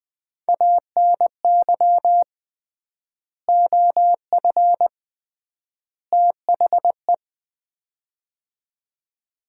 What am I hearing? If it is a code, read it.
Morse code, heard as a keyed tone, about 20 wpm, 700 Hz. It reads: ANY OF THE
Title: 3x Word-Spacing - Sets of 3 Words - 20wpm